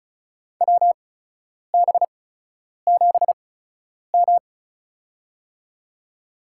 Morse code, heard as a keyed tone, about 36 wpm, spaced as for 10 wpm, 700 Hz. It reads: WB7M
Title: USA Call Signs - 4 Characters - 8x Character Spacing - 35wpm